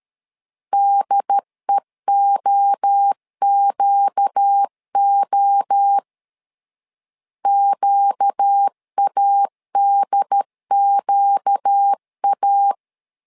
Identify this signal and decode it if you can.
Morse code, heard as a keyed tone, about 13 wpm, 800 Hz. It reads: DEOQO QADQA
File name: LEZIONI / 4 Q 12wpm 1833